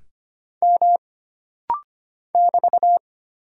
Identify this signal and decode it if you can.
Morse code, heard as a keyed tone, about 25 wpm, 700 Hz. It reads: M <BT>
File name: Single Character (Letter:Number:Punctuation:Prosign) - Rapid-Fire - 25wpm